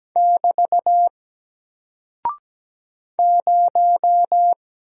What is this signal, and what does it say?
Morse code, heard as a keyed tone, about 17 wpm, 700 Hz. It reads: <BT> 0
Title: Single Character (Letter:Number:Punctuation:Prosign) 17wpm